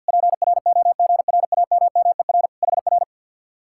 Morse code, heard as a keyed tone, about 64 wpm, 700 Hz. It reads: PROGRAMMER HR